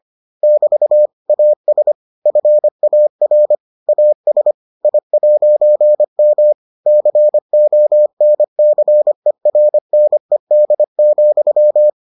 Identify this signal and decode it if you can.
Morse code, heard as a keyed tone, about 25 wpm, 600 Hz. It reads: <BT> AS FAR AS I'M CONCERNED,